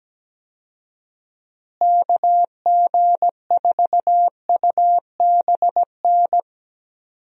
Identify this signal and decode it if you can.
Morse code, heard as a keyed tone, about 17 wpm, 700 Hz. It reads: KG4UBN
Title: USA Call Signs - 6 Characters - Repeated 3x - 17wpm